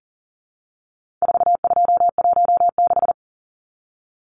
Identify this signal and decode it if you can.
Morse code, heard as a keyed tone, about 40 wpm, 700 Hz. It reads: 4216
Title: Numbers - 4-Digit 40wpm